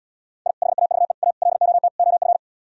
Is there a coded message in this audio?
Yes: Morse code from 0.5 to 2.4 s, heard as a keyed tone, about 77 wpm, 700 Hz. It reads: I HAVE A LOT OF